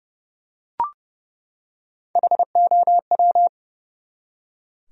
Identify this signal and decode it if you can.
Morse code, heard as a keyed tone, about 30 wpm, 700 Hz. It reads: HOW
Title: Top 100 Words 30wpm